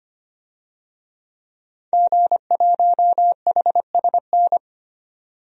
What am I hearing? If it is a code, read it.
Morse code, heard as a keyed tone, about 25 wpm, 700 Hz. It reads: G1HSN
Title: DXCC-100 Call Signs - 5 Characters - 25wpm